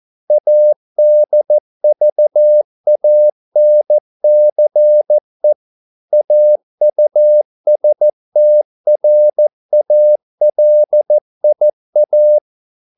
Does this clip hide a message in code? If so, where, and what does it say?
Yes: Morse code from 0.3 to 12.4 s, heard as a keyed tone, about 14 wpm, 600 Hz. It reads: ADVANCE AUSTRALIA